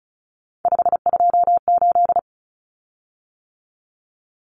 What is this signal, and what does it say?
Morse code, heard as a keyed tone, about 34 wpm, 700 Hz. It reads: 528
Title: Common Cut Numbers - 3-Digit 35wpm